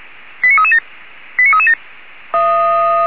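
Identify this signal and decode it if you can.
Five-tone selective call (ZVEI), heard as quick runs of five tones, tones of about 70 ms. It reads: 89337 89337
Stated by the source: ZVEI five-tone selective call received on an SDR.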